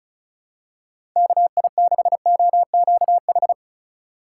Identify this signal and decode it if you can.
Morse code, heard as a keyed tone, about 36 wpm, 700 Hz. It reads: KI6OQH
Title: USA Call Signs - 6 Characters - 6x Character Spacing - 35wpm